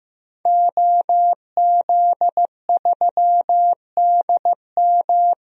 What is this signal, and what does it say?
Morse code, heard as a keyed tone, about 16 wpm, 700 Hz. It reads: OZ3DM
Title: DXCC-100 Call Signs - 5 Characters - 15wpm